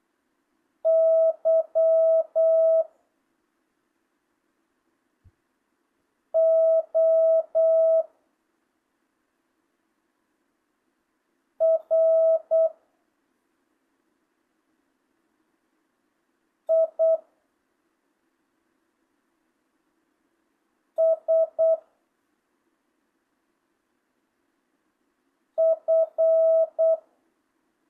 Morse code, heard as a keyed tone, about 8 wpm, 650 Hz. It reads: Y O R I S F
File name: jh8cbh / 2025cw